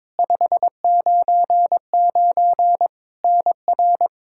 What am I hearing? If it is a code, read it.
Morse code, heard as a keyed tone, about 22 wpm, 700 Hz. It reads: 599 NR
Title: Practice QSO - Exchange 1 - Beginning - 22wpm